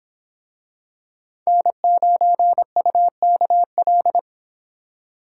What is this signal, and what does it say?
Morse code, heard as a keyed tone, about 25 wpm, 700 Hz. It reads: N9UKL